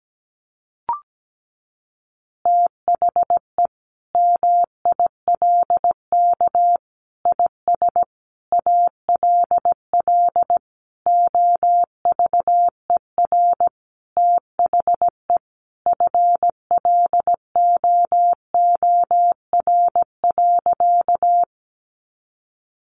Morse code, heard as a keyed tone, about 17 wpm, 700 Hz. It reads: THE MILK IS ALL OVER THE FLOOR.
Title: Additional Easy Phrases 17wpm